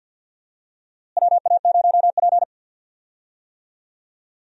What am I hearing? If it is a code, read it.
Morse code, heard as a keyed tone, about 47 wpm, 700 Hz. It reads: WA0P
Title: USA Call Signs - 4 Characters - Repeated 2x - 50wpm